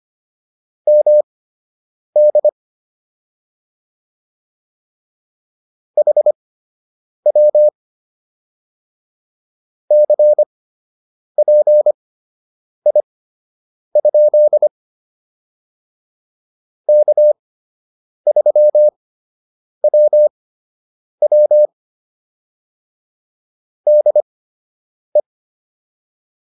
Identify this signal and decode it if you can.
Morse code, heard as a keyed tone, about 25 wpm, spaced as for 8 wpm, 600 Hz. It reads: MD HW CPI? K3WW DE